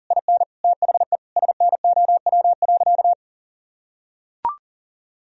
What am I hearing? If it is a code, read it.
Morse code, heard as a keyed tone, about 40 wpm, 700 Hz. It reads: IN THE SNOW.
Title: Sentences from Top 1000 Words - Review Part 3 - 40wpm